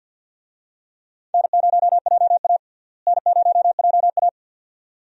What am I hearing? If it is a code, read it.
Morse code, heard as a keyed tone, about 52 wpm, 700 Hz. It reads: N0JA N0JA
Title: USA Call Signs - 4 Characters - Repeated 2x - 50wpm